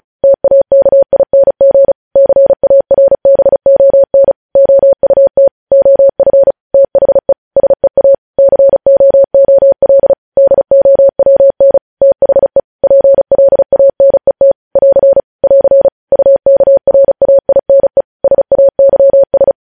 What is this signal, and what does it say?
Morse code, heard as a keyed tone, about 36 wpm, 550 Hz. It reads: TAKING CARBON OUT OF THE SEA COOL DOWN THE PLANET <AR> <AR> UKRAINE SAYS
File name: podcast / News headlines-1x-35x35